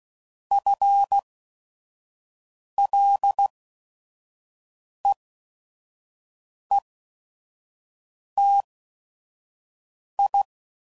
Morse code, heard as a keyed tone, about 16 wpm, spaced as for 5 wpm, 800 Hz. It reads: FLEETI
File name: Prefixes / FLs & THs / FL Prefixes 3